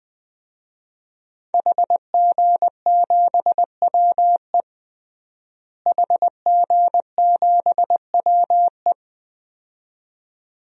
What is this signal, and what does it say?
Morse code, heard as a keyed tone, about 20 wpm, 700 Hz. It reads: HG7WE HG7WE
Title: DXCC-100 Call Signs - 5 Characters - Repeated 2x - 20wpm